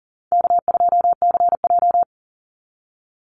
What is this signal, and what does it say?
Morse code, heard as a keyed tone, about 40 wpm, 700 Hz. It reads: K2CJ
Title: Easy Call Signs - 40wpm